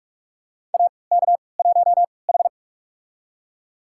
Morse code, heard as a keyed tone, about 45 wpm, spaced as for 25 wpm, 700 Hz. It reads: AK1H
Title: USA Call Signs - 3x Character Spacing - 45wpm